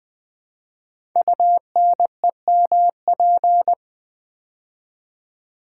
Morse code, heard as a keyed tone, about 20 wpm, 700 Hz. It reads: UNEMP
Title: Build A Word - 5 Letters At A Time - 12-Letter Words - 20wpm